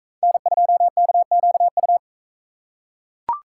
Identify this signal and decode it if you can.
Morse code, heard as a keyed tone, about 38 wpm, 700 Hz. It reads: N2KQU